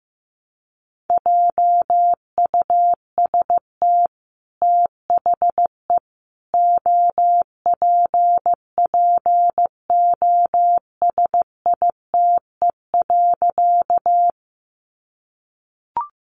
Morse code, heard as a keyed tone, about 15 wpm, 700 Hz. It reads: JUST THE OPPOSITE.
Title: Sentences from Top 1300 Words 15wpm